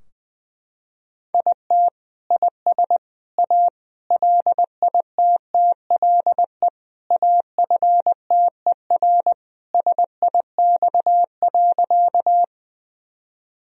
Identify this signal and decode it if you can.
Morse code, heard as a keyed tone, about 20 wpm, 700 Hz. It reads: IT IS A LITTLE AFTER SIX.
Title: Sentences from Top 1000 Words - Review Part 2 - 20wpm